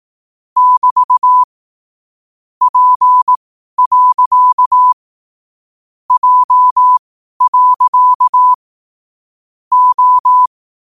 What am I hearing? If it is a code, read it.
Morse code, heard as a keyed tone, about 18 wpm, spaced as for 12 wpm, 1000 Hz. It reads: <BT> P. J. O
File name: qotdfiles / 12wpm